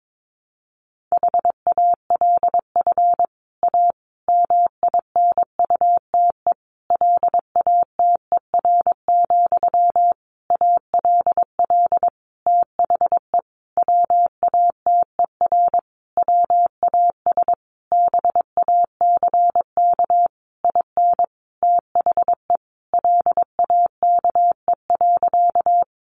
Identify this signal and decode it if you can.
Morse code, heard as a keyed tone, about 22 wpm, 700 Hz. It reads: HALF A MINUTE LATER, ALL THE WATER WAS BACK IN THE LAKE.
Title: Sentences from Top 1100 Words 22wpm